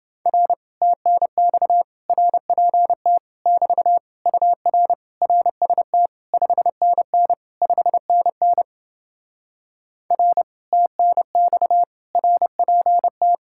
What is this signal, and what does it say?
Morse code, heard as a keyed tone, about 30 wpm, 700 Hz. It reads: R TNX RPT <BT> UR RST 5NN 5NN R TNX RPT